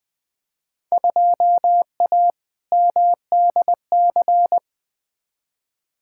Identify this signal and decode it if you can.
Morse code, heard as a keyed tone, about 20 wpm, 700 Hz. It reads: 2A MDC
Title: Field Day Exchanges - 20wpm